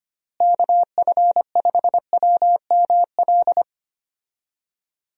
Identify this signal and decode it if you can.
Morse code, heard as a keyed tone, about 25 wpm, 700 Hz. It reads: KF5WML
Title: USA Call Signs - 6 Characters - Repeated 3x - 25wpm